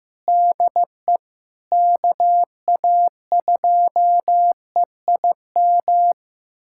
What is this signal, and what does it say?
Morse code, heard as a keyed tone, about 15 wpm, 700 Hz. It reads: DE KA2EIM
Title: USA Callsign DE Callsign - 6 Characters - 15wpm